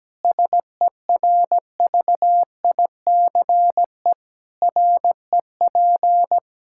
Morse code, heard as a keyed tone, about 17 wpm, 700 Hz. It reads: SERVICE REP
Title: K1USN SST QSO Practice 17wpm